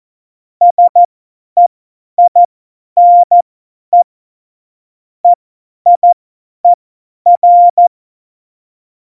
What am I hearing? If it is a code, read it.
Morse code, heard as a keyed tone, about 14 wpm, spaced as for 10 wpm, 700 Hz. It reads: SEINE EIER
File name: Morsekurs / Witze / Witz 28 April 2020, 50 F70